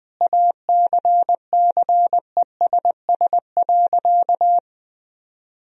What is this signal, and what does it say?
Morse code, heard as a keyed tone, about 21 wpm, 700 Hz. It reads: ACCESS.
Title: Sentences from Top 1000 Words - Review Part 4 - 20wpm